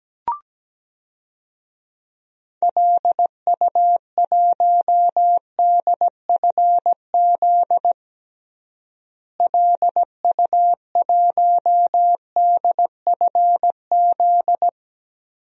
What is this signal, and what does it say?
Morse code, heard as a keyed tone, about 17 wpm, 700 Hz. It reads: LU1DFZ LU1DFZ
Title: DXCC-100 Call Signs - 6 Characters - Repeated 2x - 17wpm